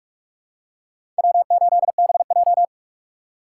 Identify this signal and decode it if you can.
Morse code, heard as a keyed tone, about 46 wpm, 700 Hz. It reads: W8BJ
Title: USA Call Signs - 4 Characters - 4x Character Spacing - 45wpm